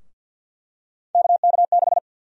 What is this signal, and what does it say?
Morse code, heard as a keyed tone, about 48 wpm, 700 Hz. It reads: KK6
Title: USA Call Sign Prefixes - 50wpm